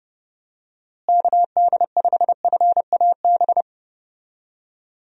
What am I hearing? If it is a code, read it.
Morse code, heard as a keyed tone, about 32 wpm, 700 Hz. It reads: KD5FAB
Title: USA Call Signs - 6 Characters - Repeated 2x - 30wpm